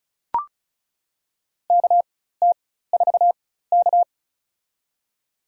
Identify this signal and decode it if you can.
Morse code, heard as a keyed tone, about 36 wpm, spaced as for 16 wpm, 700 Hz. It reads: KT4K
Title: USA Call Signs - 4 Characters - 4x Character Spacing - 35wpm